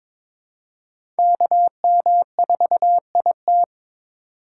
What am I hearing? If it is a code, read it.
Morse code, heard as a keyed tone, about 21 wpm, 700 Hz. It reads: KM4IT
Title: USA Call Signs - 5 Characters - 4x Character Spacing - 22wpm